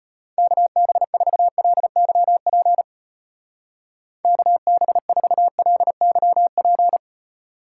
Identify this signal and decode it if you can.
Morse code, heard as a keyed tone, about 32 wpm, 700 Hz. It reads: KB4LYP KB4LYP